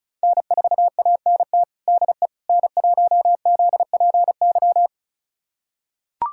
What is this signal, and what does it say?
Morse code, heard as a keyed tone, about 36 wpm, 700 Hz. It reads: N4ANT DE N1ZPY
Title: USA Callsign DE Callsign - 5 Characters - 35wpm